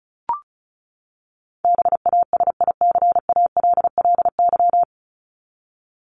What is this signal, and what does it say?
Morse code, heard as a keyed tone, about 34 wpm, 700 Hz. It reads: BASICALLY
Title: Top 1251-1500 Words 35wpm